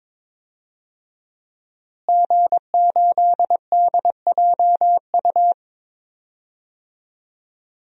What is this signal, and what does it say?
Morse code, heard as a keyed tone, about 21 wpm, 700 Hz. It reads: G8DJU
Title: DXCC-100 Call Signs - 5 Characters - Repeated 3x - 22wpm